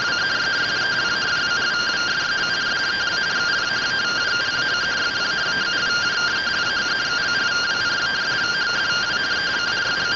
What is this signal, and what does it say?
Two-tone teleprinter signal, about 170 Hz shift.